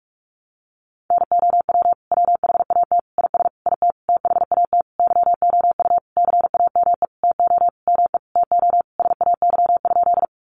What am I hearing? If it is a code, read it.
Morse code, heard as a keyed tone, about 44 wpm, 700 Hz. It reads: NOW WHAT IS IT THAT YOU CAME TO ME TO SAY?
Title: Sentences from Top 200 Words 45wpm